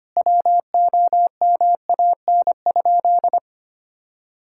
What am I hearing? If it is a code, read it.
Morse code, heard as a keyed tone, about 25 wpm, 700 Hz. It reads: WOMAN?
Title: Sentences from Top 1600 Words 25wpm